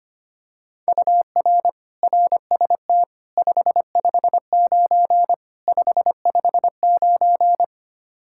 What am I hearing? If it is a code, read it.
Morse code, heard as a keyed tone, about 25 wpm, 700 Hz. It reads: UR RST 559 559
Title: Practice QSO - Exchange 1 - Beginning - 25wpm